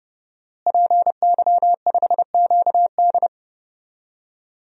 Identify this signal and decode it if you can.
Morse code, heard as a keyed tone, about 30 wpm, 700 Hz. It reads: PY5QD
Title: DXCC-100 Call Signs - 5 Characters - Repeated 3x - 30wpm